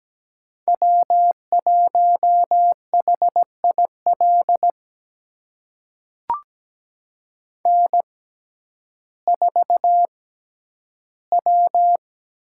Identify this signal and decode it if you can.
Morse code, heard as a keyed tone, about 17 wpm, spaced as for 10 wpm, 700 Hz. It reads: W1HIL N4W
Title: USA Call Signs - 5 Characters - 6x Character Spacing - 17wpm